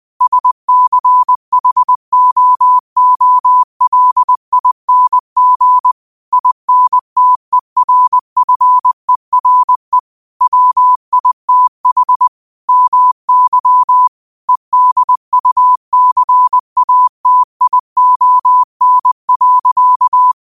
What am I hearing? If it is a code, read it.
Morse code, heard as a keyed tone, about 20 wpm, 1000 Hz. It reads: SCHOOLING INTERFERE WITH MY EDUCATION.
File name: qotdfiles / 20wpm